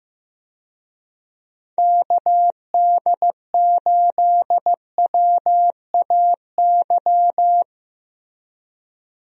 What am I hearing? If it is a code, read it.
Morse code, heard as a keyed tone, about 15 wpm, 700 Hz. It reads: KD8WAY
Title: USA Call Signs - 6 Characters - Repeated 2x - 15wpm